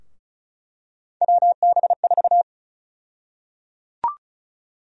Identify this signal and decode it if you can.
Morse code, heard as a keyed tone, about 36 wpm, 700 Hz. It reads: WB4
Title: USA Call Sign Prefixes - Repeated 2x - 35wpm